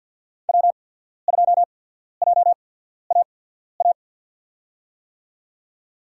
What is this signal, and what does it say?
Morse code, heard as a keyed tone, about 53 wpm, spaced as for 14 wpm, 700 Hz. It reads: W2JAA
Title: USA Call Signs - 5 Characters - 8x Character Spacing - 50wpm